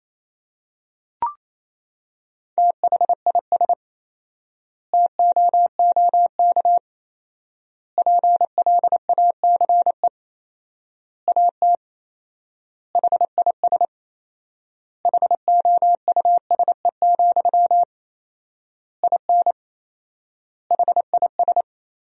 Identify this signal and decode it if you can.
Morse code, heard as a keyed tone, about 28 wpm, 700 Hz. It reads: THIS TOOK PLACE AT HIS HOUSE, IN HIS
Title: 4x Word-Spacing - Sentences from Top 300 Words - 28wpm